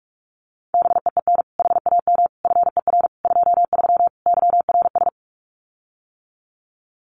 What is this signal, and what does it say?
Morse code, heard as a keyed tone, about 46 wpm, 700 Hz. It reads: BEEN HAM FER 23 YRS
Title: Common QSO Phrases Encore 45wpm